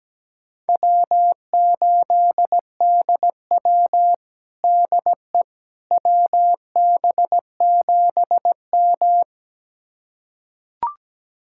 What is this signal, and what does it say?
Morse code, heard as a keyed tone, about 17 wpm, 700 Hz. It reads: W8DW DE WB7M
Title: USA Callsign DE Callsign - 4 Characters - 17wpm